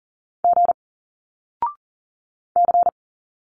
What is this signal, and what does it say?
Morse code, heard as a keyed tone, about 39 wpm, 700 Hz. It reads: G C
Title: Single Letters 40wpm